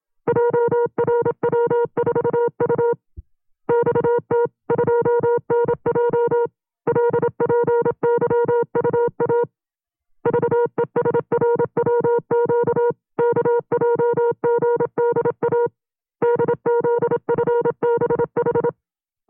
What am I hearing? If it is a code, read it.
Morse code, heard as a keyed tone, about 26 wpm, 470 Hz. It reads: JRW4U XT2NJ LPYUA VESRWQ KJGDA DZFBH
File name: CW-practice/25WPM